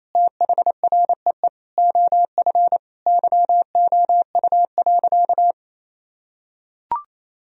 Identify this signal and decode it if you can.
Morse code, heard as a keyed tone, about 27 wpm, 700 Hz. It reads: THREE OF YOU.
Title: Sentences from Top 1000 Words - Review Part 1 - 28wpm